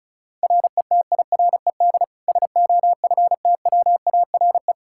Morse code, heard as a keyed tone, about 36 wpm, 700 Hz. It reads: RETIRED SOFTWARE
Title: Practice QSO - Exchange 3 - More Info - 35wpm